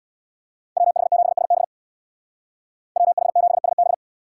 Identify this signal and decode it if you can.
Morse code, heard as a keyed tone, about 64 wpm, 700 Hz. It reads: WH7SB WH7SB